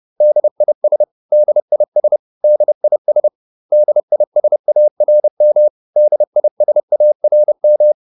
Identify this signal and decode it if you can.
Morse code, heard as a keyed tone, about 30 wpm, 600 Hz. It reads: DIS DIS DIS DISARM DISARM